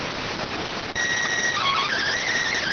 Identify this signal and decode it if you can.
SSTV (slow-scan television), PD 290.